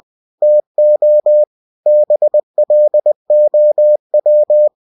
Morse code, heard as a keyed tone, about 20 wpm, 600 Hz. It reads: TO BLOW
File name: cw-books / winnie-the-pooh / winnie-20wpm-page-0007